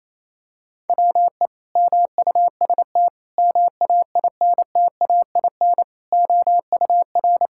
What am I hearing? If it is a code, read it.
Morse code, heard as a keyed tone, about 28 wpm, 700 Hz. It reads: WE MUST MAINTAIN OUR
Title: Sentences from Top 1000 Words - Review Part 5 - 28wpm